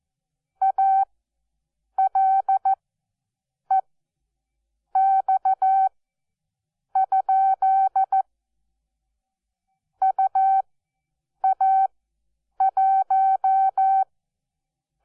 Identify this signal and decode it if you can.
Morse code, heard as a keyed tone, about 14 wpm, spaced as for 7 wpm, 770 Hz. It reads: ALEX? UA1
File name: Onlinelehrgang / morsen / DJ4UF Morsen12